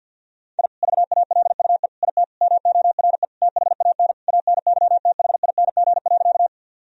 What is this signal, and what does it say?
Morse code, heard as a keyed tone, about 50 wpm, 700 Hz. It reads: I VALUE IT MORE THAN ANYTHING.
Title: Sentences from Top 1000 Words - Review Part 3 - 50wpm